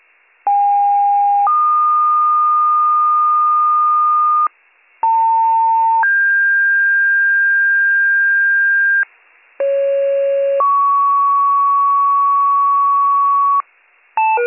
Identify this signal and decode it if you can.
Two-tone page, heard as a one-second tone, then a three-second tone, about 810 then 1220 Hz, about 900 then 1600 Hz, about 550 then 1120 Hz. It is heard from 0.5 to 13.6 s.